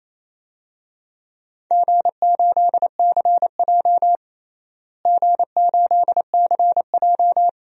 Morse code, heard as a keyed tone, about 28 wpm, 700 Hz. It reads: G8CJ G8CJ
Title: DXCC-100 Call Signs - 4 Characters - Repeated 2x - 28wpm